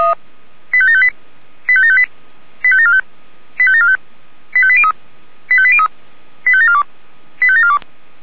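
Five-tone selective call (ZVEI), heard as quick runs of five tones, tones of about 70 ms. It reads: 85758 85758 85745 85745 85803 85803 85742 85742
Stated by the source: ZVEI five-tone selective call received by SDR.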